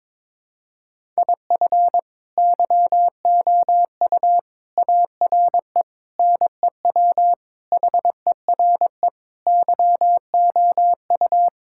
Morse code, heard as a keyed tone, about 22 wpm, 700 Hz. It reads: IF YOU ARE NEW HERE YOU
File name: practice / 22wpm / Sentences with 8-Word Phrases - 22wpm